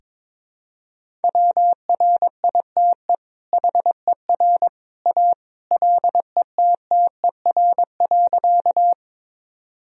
Morse code, heard as a keyed tone, about 21 wpm, 700 Hz. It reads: WRITE HER A LETTER.
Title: Sentences from Top 1000 Words - Review Part 1 - 22wpm